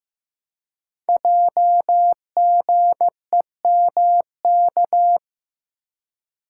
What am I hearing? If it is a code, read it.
Morse code, heard as a keyed tone, about 15 wpm, 700 Hz. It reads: JGEMK
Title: Code Groups - 5 Letters - 15wpm